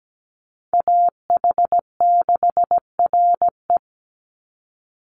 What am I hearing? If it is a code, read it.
Morse code, heard as a keyed tone, about 17 wpm, 700 Hz. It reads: AH6RE
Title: Call Signs - 17wpm